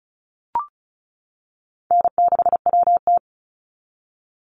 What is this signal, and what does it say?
Morse code, heard as a keyed tone, about 34 wpm, 700 Hz. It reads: N6WT
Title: NA SOTA Chaser Callsigns 35wpm